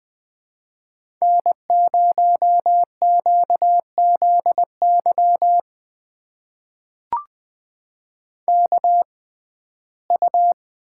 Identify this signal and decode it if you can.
Morse code, heard as a keyed tone, about 20 wpm, spaced as for 14 wpm, 700 Hz. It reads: N0QZY KU
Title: USA Call Signs - 5 Characters - 6x Character Spacing - 20wpm